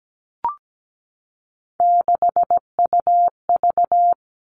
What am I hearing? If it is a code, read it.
Morse code, heard as a keyed tone, about 17 wpm, 700 Hz. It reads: 6UV
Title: All Cut Numbers - 3-Digit 17wpm